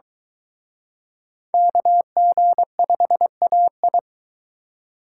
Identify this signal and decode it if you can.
Morse code, heard as a keyed tone, about 23 wpm, 700 Hz. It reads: KG5AI